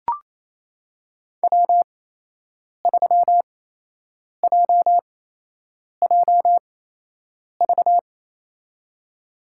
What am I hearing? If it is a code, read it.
Morse code, heard as a keyed tone, about 28 wpm, spaced as for 8 wpm, 700 Hz. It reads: W3JJV
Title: USA Call Signs - 8x Character Spacing - 28wpm